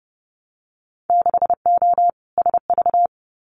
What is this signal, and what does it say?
Morse code, heard as a keyed tone, about 30 wpm, 700 Hz. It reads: 6O SV
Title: Winter Field Day Exchanges Encore 30wpm